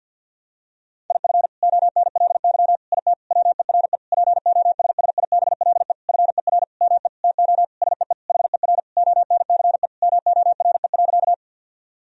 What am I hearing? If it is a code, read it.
Morse code, heard as a keyed tone, about 52 wpm, 700 Hz. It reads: IF ONLY IT WERE POSSIBLE FER ME TO SEE HER ONCE MORE.
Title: Sentences from Top 500 Words and Abbreviations 50wpm